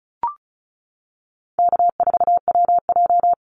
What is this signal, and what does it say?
Morse code, heard as a keyed tone, about 36 wpm, 700 Hz. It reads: K4WJ
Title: Call Signs - 35wpm